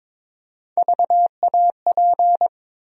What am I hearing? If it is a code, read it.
Morse code, heard as a keyed tone, about 22 wpm, 700 Hz. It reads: VAP
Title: USA Call Sign Postfixes - 3 Letters 22wpm